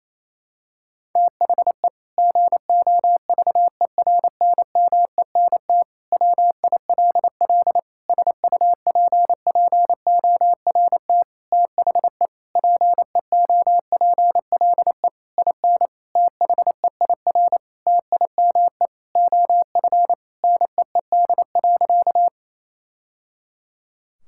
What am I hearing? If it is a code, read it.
Morse code, heard as a keyed tone, about 27 wpm, 700 Hz. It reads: THE GOVERNMENT WILL SUPPORT THE PEOPLE IN THEIR TIME OF NEED.
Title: Sentences from Top 1000 Words - Review Part 5 - 28wpm